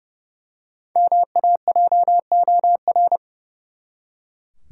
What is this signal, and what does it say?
Morse code, heard as a keyed tone, about 30 wpm, 700 Hz. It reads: MAJOR
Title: Five Letter Words - 30wpm